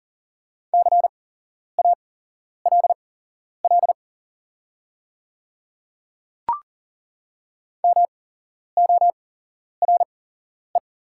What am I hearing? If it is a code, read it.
Morse code, heard as a keyed tone, about 38 wpm, spaced as for 11 wpm, 700 Hz. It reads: CALL MORE
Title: 8x Character Spacing - Top 300 Words - 40wpm